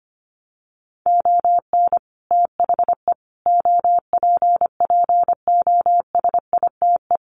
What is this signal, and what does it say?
Morse code, heard as a keyed tone, about 25 wpm, 700 Hz. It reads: ON THE OPPOSITE
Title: Sets of 5 Words Encore2 25wpm